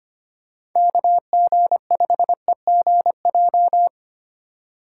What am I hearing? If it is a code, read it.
Morse code, heard as a keyed tone, about 25 wpm, 700 Hz. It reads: KG5EGJ
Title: USA Call Signs - 2x Character Spacing - 25wpm